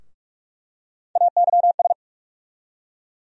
Morse code, heard as a keyed tone, about 42 wpm, 700 Hz. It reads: AYS
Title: USA Call Sign Postfixes - 3 Letters 45wpm